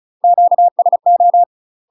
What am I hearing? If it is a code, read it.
Morse code, heard as a keyed tone, about 34 wpm, 700 Hz. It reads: QSO
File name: UK MASTER QSOs TRAINING / 35WPM / UK MASTER QSOs 0042